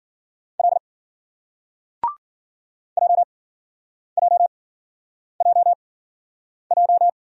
Morse code, heard as a keyed tone, about 53 wpm, spaced as for 9 wpm, 700 Hz. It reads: L JJJJ